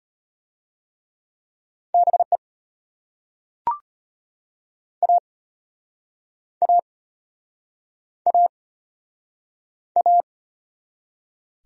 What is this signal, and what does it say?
Morse code, heard as a keyed tone, about 35 wpm, 700 Hz. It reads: BE A A A A